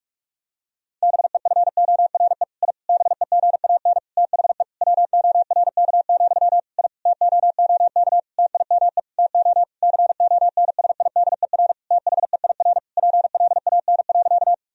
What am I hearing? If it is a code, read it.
Morse code, heard as a keyed tone, about 46 wpm, 700 Hz. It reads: BEFORE I BEGAN THE WORK, I TOOK TIME TO CONSIDER THEIR PLAN.
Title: Sentences from Top 1000 Words - Review Part 4 - 45wpm